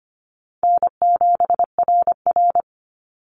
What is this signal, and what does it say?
Morse code, heard as a keyed tone, about 25 wpm, 700 Hz. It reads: N7RR
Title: Easy Call Signs - 25wpm